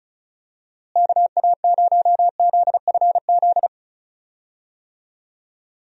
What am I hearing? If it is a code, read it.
Morse code, heard as a keyed tone, about 34 wpm, 700 Hz. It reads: KA0ZFZ
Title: USA Call Signs - 6 Characters - Repeated 2x - 35wpm